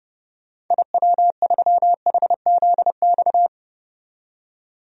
Morse code, heard as a keyed tone, about 32 wpm, 700 Hz. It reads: IW3HZX
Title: DXCC-100 Call Signs - 6 Characters - 30wpm